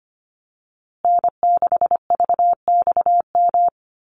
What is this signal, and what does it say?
Morse code, heard as a keyed tone, about 25 wpm, 700 Hz. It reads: N6VXM
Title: USA Call Signs Repeated 3x 25wpm